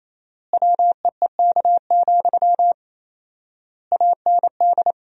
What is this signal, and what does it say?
Morse code, heard as a keyed tone, about 28 wpm, 700 Hz. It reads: WEEK, AND
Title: 4x Word-Spacing - Sentences from Top 500 Words - 28wpm